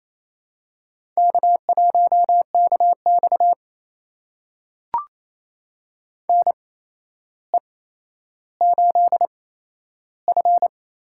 Morse code, heard as a keyed tone, about 28 wpm, spaced as for 12 wpm, 700 Hz. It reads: K1KX NE8F
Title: USA Call Signs - 4 Characters - 8x Character Spacing - 28wpm